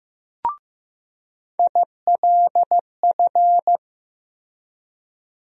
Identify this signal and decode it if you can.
Morse code, heard as a keyed tone, about 15 wpm, 700 Hz. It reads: ILF
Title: USA Call Sign Postfixes - 3 Letters 15wpm